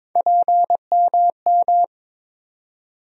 Morse code, heard as a keyed tone, about 22 wpm, 700 Hz. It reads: PMM
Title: USA Call Sign Postfixes - 3 Letters 22wpm